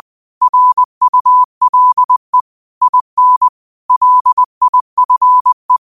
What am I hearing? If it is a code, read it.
Morse code, heard as a keyed tone, about 20 wpm, 1000 Hz. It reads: RULE IN LIFE